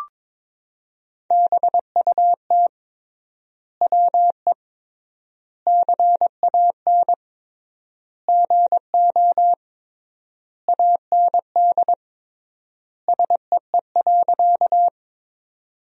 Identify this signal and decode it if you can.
Morse code, heard as a keyed tone, about 22 wpm, 700 Hz. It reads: BUT WE CAN GO AND SEE.
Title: 3x Word-Spacing - Sentences from Top 100 Words - 22wpm